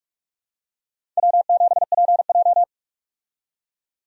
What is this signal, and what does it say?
Morse code, heard as a keyed tone, about 46 wpm, 700 Hz. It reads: W7PJ
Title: USA Call Signs - 4 Characters - 45wpm